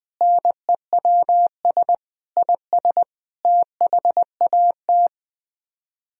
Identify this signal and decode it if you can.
Morse code, heard as a keyed tone, about 20 wpm, 700 Hz. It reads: NEWS IS THAT